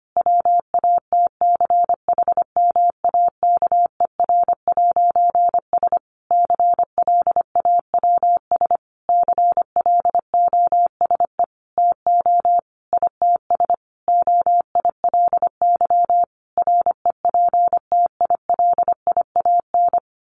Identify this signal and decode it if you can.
Morse code, heard as a keyed tone, about 25 wpm, 700 Hz. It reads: WATCHMAKER'S CLAWS CLOSE TO ITS OILY REPTILIAN
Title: A Sound Of Thunder 25wpm